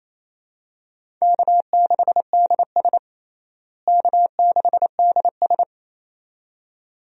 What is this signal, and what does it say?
Morse code, heard as a keyed tone, about 27 wpm, 700 Hz. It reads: K6DS K6DS
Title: USA Call Signs - 4 Characters - Repeated 2x - 28wpm